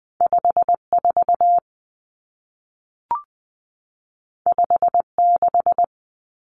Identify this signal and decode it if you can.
Morse code, heard as a keyed tone, about 20 wpm, 700 Hz. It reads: 54 56
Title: Common Cut Numbers - 2-Digit 20wpm